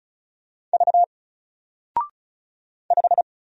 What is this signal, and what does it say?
Morse code, heard as a keyed tone, about 35 wpm, 700 Hz. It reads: V 5
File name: Single-Letter-Number-35wpm